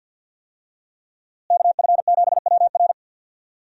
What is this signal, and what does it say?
Morse code, heard as a keyed tone, about 49 wpm, 700 Hz. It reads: KF7WR
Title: USA Call Signs - 5 Characters - Repeated 3x - 50wpm